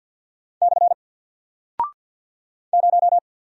Morse code, heard as a keyed tone, about 48 wpm, 700 Hz. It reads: / 0
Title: Single Character (Letter:Number:Punctuation:Prosign) Alt 50wpm